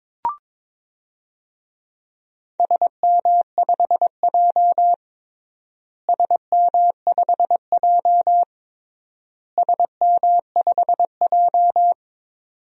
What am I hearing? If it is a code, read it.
Morse code, heard as a keyed tone, about 22 wpm, 700 Hz. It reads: SM5J SM5J SM5J
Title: DXCC-100 Call Signs - 4 Characters - Repeated 3x - 22wpm